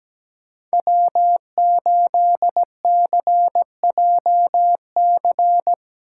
Morse code, heard as a keyed tone, about 17 wpm, 700 Hz. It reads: W8CJC
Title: USA Call Signs - 5 Characters - 3x Character Spacing - 17wpm